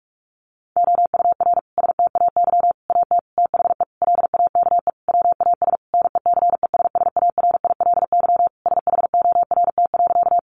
Morse code, heard as a keyed tone, about 44 wpm, 700 Hz. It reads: OUR STAY AT THE LAKE WAS NECESSARILY SHORT.
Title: Sentences from Top 1400 Words 45wpm